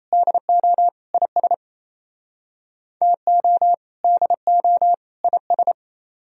Morse code, heard as a keyed tone, about 37 wpm, 700 Hz. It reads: DO IS TO DO IS